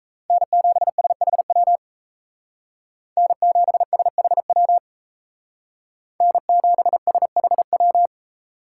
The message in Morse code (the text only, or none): N7SHW N7SHW N7SHW